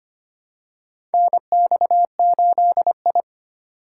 Morse code, heard as a keyed tone, about 25 wpm, 700 Hz. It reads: NX8I
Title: USA Call Signs - 4 Characters - 2x Character Spacing - 25wpm